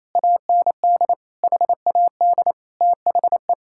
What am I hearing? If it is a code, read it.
Morse code, heard as a keyed tone, about 28 wpm, 700 Hz. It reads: AND HAD THE
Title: Sentences from Top 1000 Words - Review Part 5 - 28wpm